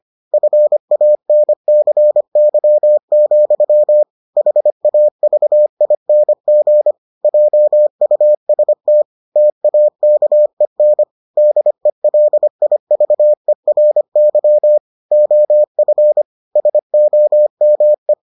Morse code, heard as a keyed tone, about 25 wpm, 600 Hz. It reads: FANCY, HAVING JUST TAKEN DELIVERY OF SOME